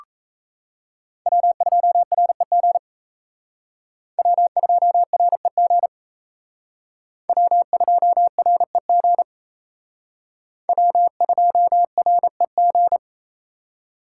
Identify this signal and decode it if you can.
Morse code, heard as a keyed tone, about 36 wpm, 700 Hz. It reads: W2REG W2REG W2REG W2REG